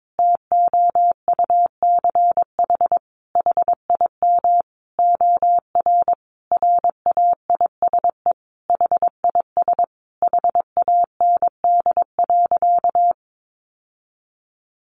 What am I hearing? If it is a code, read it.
Morse code, heard as a keyed tone, about 22 wpm, 700 Hz. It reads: TOUCH HIM OR RAISE HIS HAND.
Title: Sentences from Top 1400 Words 22wpm